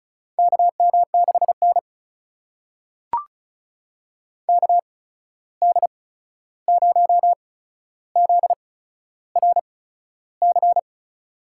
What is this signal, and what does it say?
Morse code, heard as a keyed tone, about 34 wpm, spaced as for 13 wpm, 700 Hz. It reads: KM6N KD0ZRC
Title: USA Call Signs - 8x Character Spacing - 35wpm